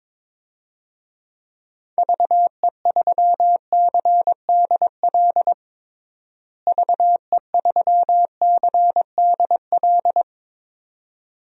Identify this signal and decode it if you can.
Morse code, heard as a keyed tone, about 22 wpm, 700 Hz. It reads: VE3CDL VE3CDL
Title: DXCC-100 Call Signs - 6 Characters - Repeated 2x - 22wpm